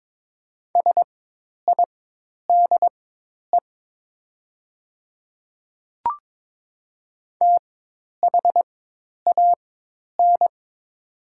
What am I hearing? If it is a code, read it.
Morse code, heard as a keyed tone, about 22 wpm, spaced as for 10 wpm, 700 Hz. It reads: SIDE THAN
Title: 4x Character Spacing - Top 200 Words - 22wpm